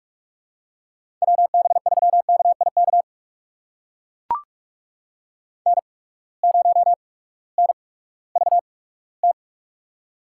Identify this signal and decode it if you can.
Morse code, heard as a keyed tone, about 44 wpm, spaced as for 20 wpm, 700 Hz. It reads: WB3KIK N0NVT
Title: USA Call Signs - 8x Character Spacing - 45wpm